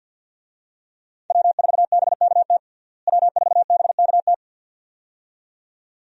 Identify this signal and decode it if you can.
Morse code, heard as a keyed tone, about 48 wpm, 700 Hz. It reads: W4BKT W4BKT
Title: USA Call Signs - 5 Characters - Repeated 2x - 50wpm